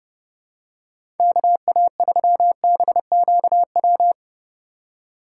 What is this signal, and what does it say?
Morse code, heard as a keyed tone, about 30 wpm, 700 Hz. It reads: KA3BQW
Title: USA Call Signs - 6 Characters - 4x Character Spacing - 30wpm